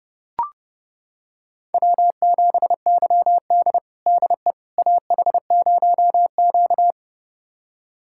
Morse code, heard as a keyed tone, about 30 wpm, 700 Hz. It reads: W7YD DE AH0Q
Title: USA Callsign DE Callsign - 4 Characters - 30wpm